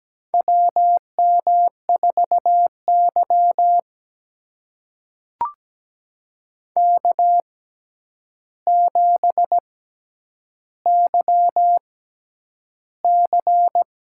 Morse code, heard as a keyed tone, about 17 wpm, spaced as for 9 wpm, 700 Hz. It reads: WM4Y K7YC
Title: USA Call Signs - 4 Characters - 6x Character Spacing - 17wpm